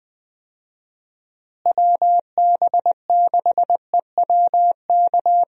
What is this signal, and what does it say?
Morse code, heard as a keyed tone, about 20 wpm, 700 Hz. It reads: WB6EWK
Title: USA Call Signs - 6 Characters - Repeated 3x - 20wpm